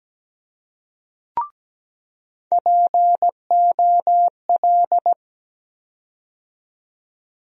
Morse code, heard as a keyed tone, about 17 wpm, 700 Hz. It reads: POL
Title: USA Call Sign Postfixes - 3 Letters 17wpm